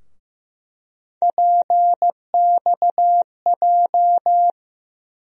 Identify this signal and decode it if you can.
Morse code, heard as a keyed tone, about 15 wpm, 700 Hz. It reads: PXJ
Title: USA Call Sign Postfixes - 3 Letters 15wpm